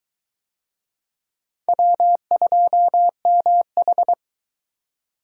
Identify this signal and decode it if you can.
Morse code, heard as a keyed tone, about 22 wpm, 700 Hz. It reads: W2MH